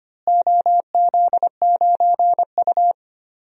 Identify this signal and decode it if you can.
Morse code, heard as a keyed tone, about 25 wpm, 700 Hz. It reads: OZ9U
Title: DXCC-100 Call Signs - 4 Characters - Repeated 3x - 25wpm